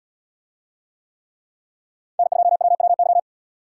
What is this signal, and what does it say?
Morse code, heard as a keyed tone, about 77 wpm, 700 Hz. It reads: N2WWY